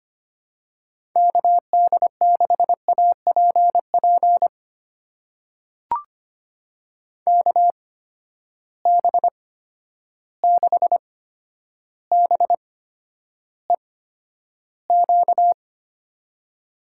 Morse code, heard as a keyed tone, about 25 wpm, spaced as for 11 wpm, 700 Hz. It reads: KD6APP KB6BEQ
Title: USA Call Signs - 6 Characters - 8x Character Spacing - 25wpm